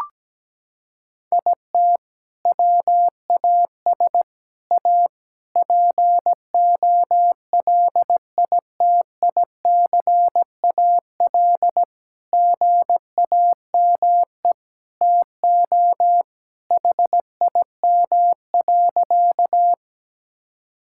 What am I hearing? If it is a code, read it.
Morse code, heard as a keyed tone, about 17 wpm, 700 Hz. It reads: IT WAS A POLITICAL GAME TO HIM.
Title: Sentences from Top 1000 Words - Review Part 5 - 17wpm